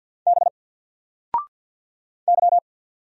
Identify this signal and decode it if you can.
Morse code, heard as a keyed tone, about 48 wpm, 700 Hz. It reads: B Y
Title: Single Character (Letter:Number:Punctuation:Prosign) Alt 50wpm